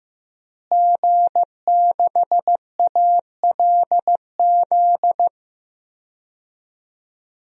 Morse code, heard as a keyed tone, about 15 wpm, 700 Hz. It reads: G6ALZ
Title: DXCC-100 Call Signs - 5 Characters - Repeated 2x - 15wpm